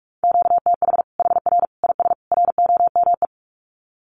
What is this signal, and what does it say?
Morse code, heard as a keyed tone, about 44 wpm, 700 Hz. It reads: QTH HR IS ROME
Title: Common QSO Phrases 45wpm